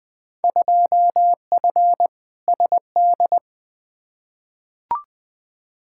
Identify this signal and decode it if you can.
Morse code, heard as a keyed tone, about 20 wpm, 700 Hz. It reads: 2F SD
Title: Field Day Exchanges Encore - 20wpm